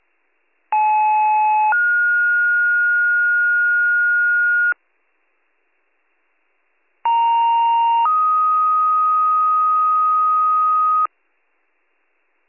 Two-tone page, heard as a one-second tone, then a three-second tone, about 860 then 1430 Hz, about 930 then 1250 Hz. It is heard from 0.7 to 11.1 s.